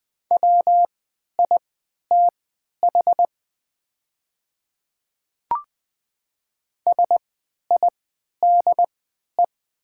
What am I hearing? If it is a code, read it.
Morse code, heard as a keyed tone, about 20 wpm, spaced as for 11 wpm, 700 Hz. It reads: WITH SIDE
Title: 3x Character Spacing - Top 200 Words - 20wpm